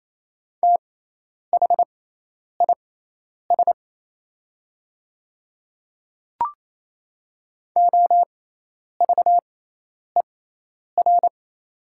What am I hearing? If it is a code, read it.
Morse code, heard as a keyed tone, about 28 wpm, spaced as for 10 wpm, 700 Hz. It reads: THIS OVER
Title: 6x Character Spacing - Top 100 Words - 28wpm